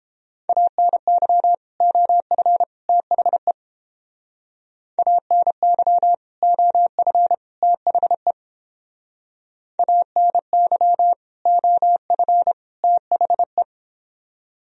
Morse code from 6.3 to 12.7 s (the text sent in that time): OF THE ANY OF